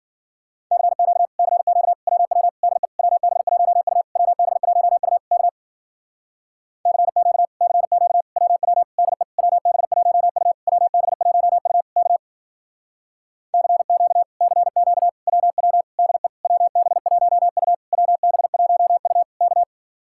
Morse code, heard as a keyed tone, about 53 wpm, 700 Hz. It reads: CQ CQ WW DE WB1U WB1U K CQ CQ WW DE WB1U WB1U K CQ CQ WW DE WB1U WB1U K